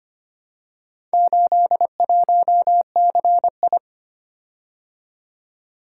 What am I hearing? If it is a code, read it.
Morse code, heard as a keyed tone, about 25 wpm, 700 Hz. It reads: 81CI
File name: practice / 25wpm / Four Letter-Number Combinations - 25wpm